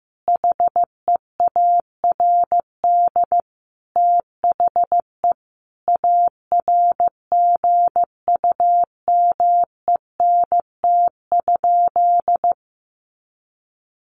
Morse code, heard as a keyed tone, about 15 wpm, 700 Hz. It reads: HEARD THE ARGUMENT?
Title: Sentences from Top 1400 Words 15wpm